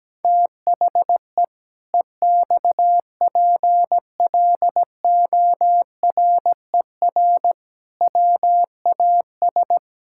Morse code, heard as a keyed tone, about 17 wpm, 700 Hz. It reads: THE EXPLORER WAS
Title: Pangram Challenge v2 - 17wpm